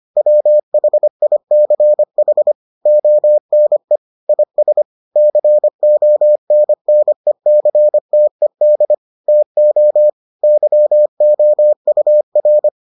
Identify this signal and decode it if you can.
Morse code, heard as a keyed tone, about 25 wpm, 600 Hz. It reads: WHICH ONE IS CONNECTED TO YOUR